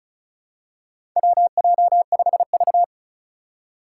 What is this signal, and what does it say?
Morse code, heard as a keyed tone, about 36 wpm, 700 Hz. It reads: WJ5V
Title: Four Challenging Letter-Number Combinations 35wpm